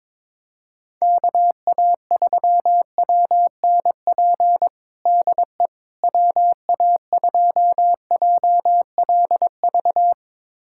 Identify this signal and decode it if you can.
Morse code, heard as a keyed tone, about 22 wpm, 700 Hz. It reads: KA3WNP DE WA2JLV
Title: USA Callsign DE Callsign - 6 Characters - 22wpm